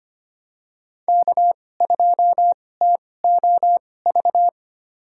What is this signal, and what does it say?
Morse code, heard as a keyed tone, about 25 wpm, spaced as for 18 wpm, 700 Hz. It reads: K2TOV
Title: USA Call Signs - 5 Characters - 2x Character Spacing - 25wpm